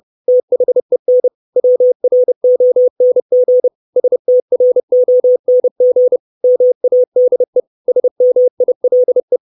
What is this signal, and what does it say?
Morse code, heard as a keyed tone, about 30 wpm, 500 Hz. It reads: THEN WRONG STRONG MADE SMILE
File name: 632 Words 30WPM